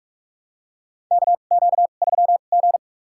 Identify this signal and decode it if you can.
Morse code, heard as a keyed tone, about 47 wpm, spaced as for 33 wpm, 700 Hz. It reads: KQ3G
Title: USA Call Signs - 4 Characters - 2x Character Spacing - 45wpm